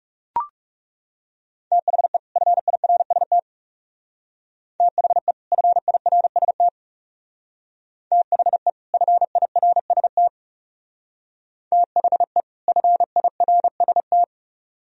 Morse code, heard as a keyed tone, about 37 wpm, 700 Hz. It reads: THE FIRST THE FIRST THE FIRST THE FIRST
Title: Speed-Racing - Sets of 2 Words 30wpm